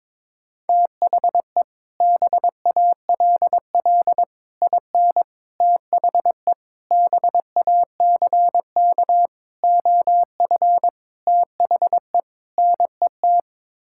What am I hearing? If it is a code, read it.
Morse code, heard as a keyed tone, about 22 wpm, 700 Hz. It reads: THE BALL IN THE BACK OF THE NET
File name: practice / 22wpm / Sets of 8 Words - 22wpm